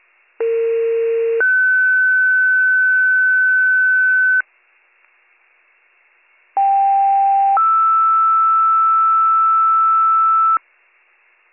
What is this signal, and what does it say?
Two-tone page, heard as a one-second tone, then a three-second tone, about 460 then 1500 Hz, about 790 then 1320 Hz. It is heard from 0.4 to 10.6 s.